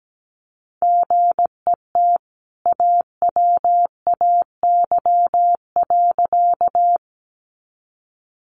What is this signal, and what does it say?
Morse code, heard as a keyed tone, about 17 wpm, 700 Hz. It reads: GET AWAY.
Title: Sentences from Top 600 Words 17wpm